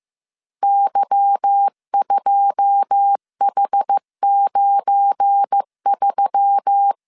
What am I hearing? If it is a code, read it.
Morse code, heard as a keyed tone, about 15 wpm, 800 Hz. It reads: Y2H93
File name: CW / DET / 1 9 1500C S A 14wpm 3222